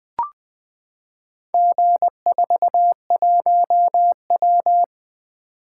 Morse code, heard as a keyed tone, about 20 wpm, 700 Hz. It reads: G41W
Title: Four Letter-Number Combinations - 20wpm